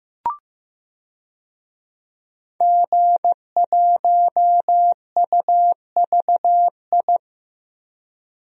Morse code, heard as a keyed tone, about 15 wpm, 700 Hz. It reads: G1UVI
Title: DXCC-100 Call Signs - 5 Characters - 15wpm